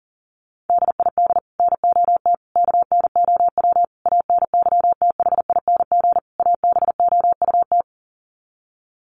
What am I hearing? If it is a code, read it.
Morse code, heard as a keyed tone, about 40 wpm, 700 Hz. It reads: DID NOT KNOW ANYTHING ABOUT
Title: Sets of 5 Words Encore2 40wpm